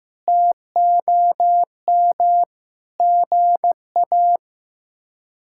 Morse code, heard as a keyed tone, about 15 wpm, 700 Hz. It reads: TOM GA
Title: K1USN SST QSO Practice - 15wpm